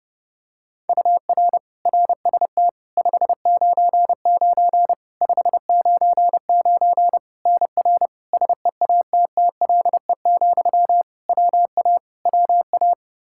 Morse code, heard as a keyed tone, about 32 wpm, 700 Hz. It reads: UR RST 599 599 NR SEATTLE, WA WA
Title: Practice QSO - Exchange 1 - Beginning - 30wpm